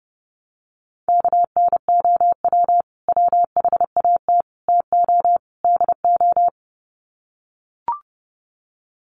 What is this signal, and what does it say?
Morse code, heard as a keyed tone, about 30 wpm, 700 Hz. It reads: KNOW WHAT TO DO
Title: Sets of 4 Words 30wpm